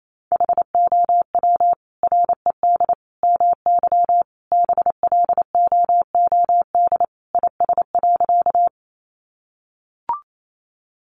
Morse code, heard as a keyed tone, about 28 wpm, 700 Hz. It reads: HOW RED MY BLOOD IS.
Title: Sentences from Top 1100 Words 28wpm